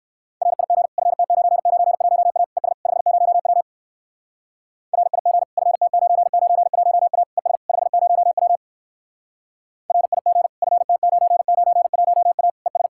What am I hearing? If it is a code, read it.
Morse code, heard as a keyed tone, about 54 wpm, 700 Hz. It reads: RIG FT991A ES 50W RIG FT991A ES 50W RIG FT991A ES
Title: Speed-Racing Common QSO Phrases Encore - 45wpm